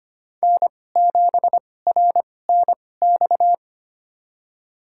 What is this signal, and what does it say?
Morse code, heard as a keyed tone, about 25 wpm, spaced as for 18 wpm, 700 Hz. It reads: N7RNX
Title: USA Call Signs - 5 Characters - 2x Character Spacing - 25wpm